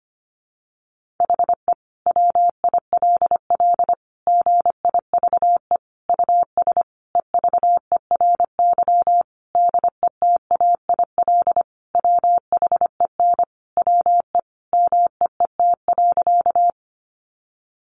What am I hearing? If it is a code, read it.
Morse code, heard as a keyed tone, about 25 wpm, 700 Hz. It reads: HE WILL GIVE US EVERY DETAIL WHEN WE MEET.
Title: Sentences from Top 1500 Words 25wpm